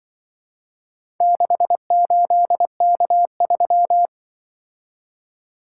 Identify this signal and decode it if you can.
Morse code, heard as a keyed tone, about 24 wpm, 700 Hz. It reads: 68K3
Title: Speed-Racing - Four Letter-Number Combinations - 20wpm